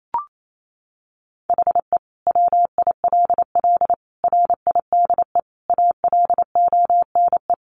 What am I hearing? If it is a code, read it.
Morse code, heard as a keyed tone, about 28 wpm, 700 Hz. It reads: HE WILL RIDE ALONE
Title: Sentences from Top 1400 Words 28wpm